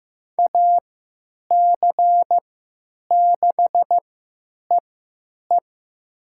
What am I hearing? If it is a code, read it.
Morse code, heard as a keyed tone, about 15 wpm, spaced as for 9 wpm, 700 Hz. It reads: AC6EE
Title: USA Call Signs - 5 Characters - 3x Character Spacing - 15wpm